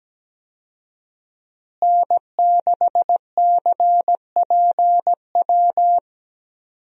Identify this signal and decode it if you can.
Morse code, heard as a keyed tone, about 17 wpm, 700 Hz. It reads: N6CPW
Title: USA Call Signs - 5 Characters - Repeated 2x - 17wpm